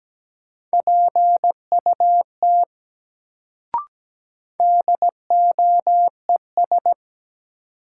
Morse code, heard as a keyed tone, about 17 wpm, 700 Hz. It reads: PUT DOES
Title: Sentences from Top 200 Words - New Word Drill - 17wpm